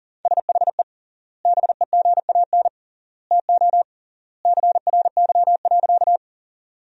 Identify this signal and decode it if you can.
Morse code, heard as a keyed tone, about 40 wpm, 700 Hz. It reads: SHE BEGAN TO CRY.
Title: 3x Word-Spacing - Sentences from Top 500 Words - 40wpm